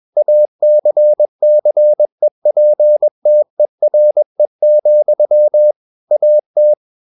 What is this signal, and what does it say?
Morse code, heard as a keyed tone, about 21 wpm, 600 Hz. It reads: ACCEPTERE, AT
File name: Ferie QRS 21wpm